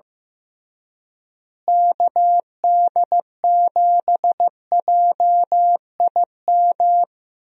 Morse code, heard as a keyed tone, about 15 wpm, 700 Hz. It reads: KD7JIM
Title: USA Call Signs - 6 Characters - Repeated 3x - 15wpm